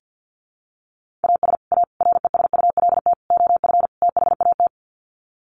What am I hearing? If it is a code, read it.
Morse code, heard as a keyed tone, about 50 wpm, 700 Hz. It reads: AS A RESULT OF THAT
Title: Sets of 5 Words Encore 50wpm